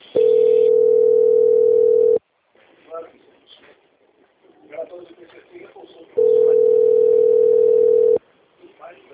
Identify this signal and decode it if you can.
Ringback tone (North America), 440 + 480 Hz, with speech.
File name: out- - - - - .24704